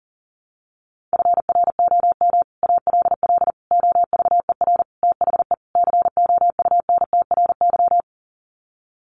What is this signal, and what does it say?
Morse code, heard as a keyed tone, about 40 wpm, 700 Hz. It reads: FROM ALL OVER THE COUNTRY
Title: Sets of 5 Words Encore 40wpm